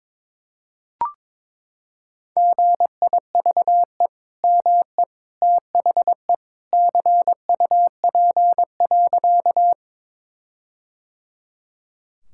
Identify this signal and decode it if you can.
Morse code, heard as a keyed tone, about 21 wpm, 700 Hz. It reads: GIVE ME THE CUP.
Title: Sentences from Top 1600 Words 22wpm